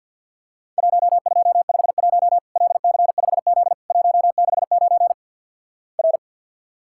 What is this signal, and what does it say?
Morse code, heard as a keyed tone, about 50 wpm, 700 Hz. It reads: 1251 LC5Z 169 R
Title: Worked All Europe DX Contest - QTC Exchange - 50wpm